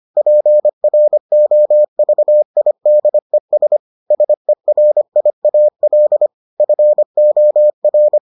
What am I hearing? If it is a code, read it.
Morse code, heard as a keyed tone, about 25 wpm, 600 Hz. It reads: PROVIDES SERIAL FOR